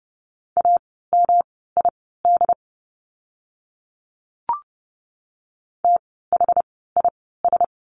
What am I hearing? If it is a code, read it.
Morse code, heard as a keyed tone, about 30 wpm, spaced as for 17 wpm, 700 Hz. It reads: AMID THIS
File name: course / 30:10wpm / 108 Lesson 18f - Letter m - Words 30wpm